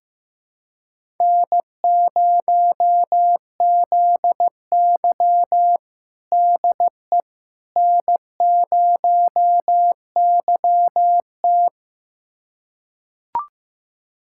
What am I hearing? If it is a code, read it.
Morse code, heard as a keyed tone, about 15 wpm, 700 Hz. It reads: N0ZY DE N0YT
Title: USA Callsign DE Callsign - 4 Characters - 15wpm